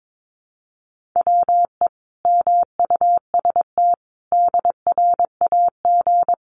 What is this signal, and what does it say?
Morse code, heard as a keyed tone, about 22 wpm, 700 Hz. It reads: WE MUST DRAG
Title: Sentences from Top 1500 Words 22wpm